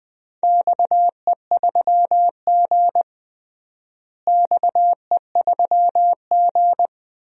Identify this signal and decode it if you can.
Morse code, heard as a keyed tone, about 20 wpm, 700 Hz. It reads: XE3G XE3G
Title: DXCC-100 Call Signs - 4 Characters - Repeated 3x - 20wpm